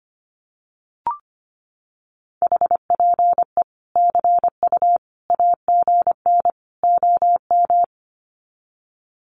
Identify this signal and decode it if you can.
Morse code, heard as a keyed tone, about 25 wpm, 700 Hz. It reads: HPE CU AGN OM
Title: Common QSO Phrases 25wpm